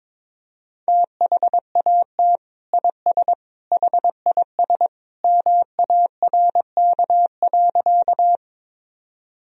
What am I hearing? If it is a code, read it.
Morse code, heard as a keyed tone, about 21 wpm, 700 Hz. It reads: THAT IS HIS MARK.
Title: Sentences from Top 1000 Words - Review Part 1 - 22wpm